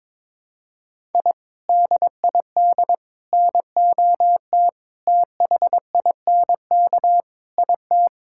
Morse code, heard as a keyed tone, about 22 wpm, 700 Hz. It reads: I DID NOT THINK IT
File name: practice / 22wpm / Sentences with 4-Word Phrases - 22wpm